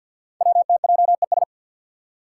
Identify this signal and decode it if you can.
Morse code, heard as a keyed tone, about 47 wpm, 690 Hz. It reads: WTJES
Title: Code Groups - 5 Letters - 50wpm